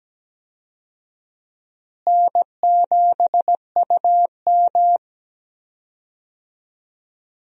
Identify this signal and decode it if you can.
Morse code, heard as a keyed tone, about 17 wpm, 700 Hz. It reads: N7UM
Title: USA Call Signs - 4 Characters - Repeated 3x - 17wpm